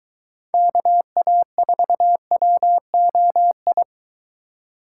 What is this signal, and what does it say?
Morse code, heard as a keyed tone, about 22 wpm, 700 Hz. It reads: KA4WOI